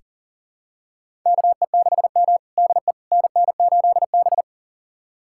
Morse code, heard as a keyed tone, about 37 wpm, 700 Hz. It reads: KE6M DE NN8B
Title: USA Callsign DE Callsign - 4 Characters - 40wpm